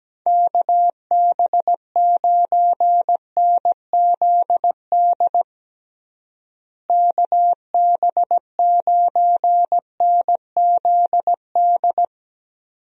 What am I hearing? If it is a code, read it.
Morse code, heard as a keyed tone, about 17 wpm, 700 Hz. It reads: KB9NZD KB9NZD
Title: USA Call Signs - 6 Characters - Repeated 2x - 17wpm